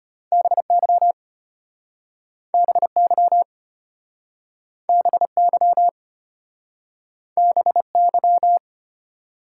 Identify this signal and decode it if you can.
Morse code, heard as a keyed tone, about 33 wpm, 700 Hz. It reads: BY BY BY BY